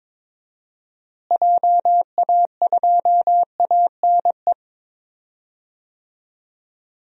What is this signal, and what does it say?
Morse code, heard as a keyed tone, about 22 wpm, 700 Hz. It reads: JA2ANE
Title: DXCC-100 Call Signs - 6 Characters - Repeated 3x - 22wpm